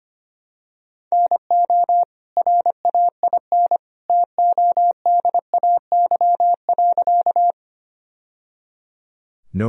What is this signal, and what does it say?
Morse code, heard as a keyed tone, about 25 wpm, 700 Hz. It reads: NO RAIN TODAY.
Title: Sentences from Top 1000 Words - Review Part 1 - 25wpm